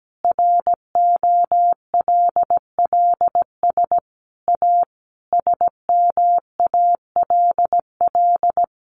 Morse code, heard as a keyed tone, about 17 wpm, 700 Hz. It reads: ROLLS A SMALL
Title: Easy Phrases 17wpm